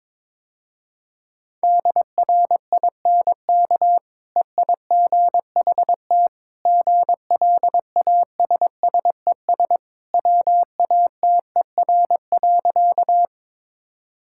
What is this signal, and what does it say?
Morse code, heard as a keyed tone, about 22 wpm, 700 Hz. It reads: DRINK EIGHT GLASSES WATER.
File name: practice / 22wpm / Easy Sentences with Hints - 22wpm